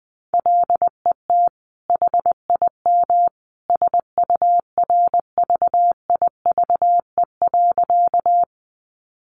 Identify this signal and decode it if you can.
Morse code, heard as a keyed tone, about 20 wpm, 700 Hz. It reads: LET HIM SURVIVE.
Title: Sentences from Top 1300 Words 20wpm